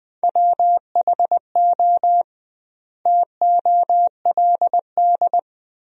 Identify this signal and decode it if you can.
Morse code, heard as a keyed tone, about 20 wpm, 700 Hz. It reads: WHO TOLD
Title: 2x Word-Spacing - Sentences from Top 400 Words - 20wpm